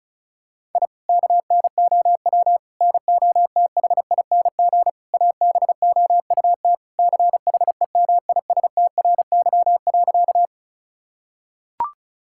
Morse code, heard as a keyed tone, about 36 wpm, 700 Hz. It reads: I KNOW NOTHING ABOUT CHEMISTRY.
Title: Sentences from Top 1000 Words - Review Part 6 - 35wpm